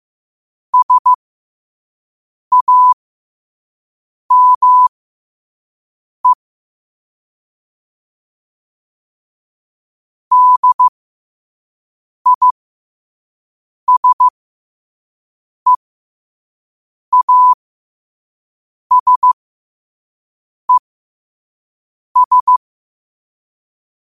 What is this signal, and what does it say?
Morse code, heard as a keyed tone, about 15 wpm, spaced as for 5 wpm, 1000 Hz. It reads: SAME DISEASES
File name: qotdfiles / 5wpm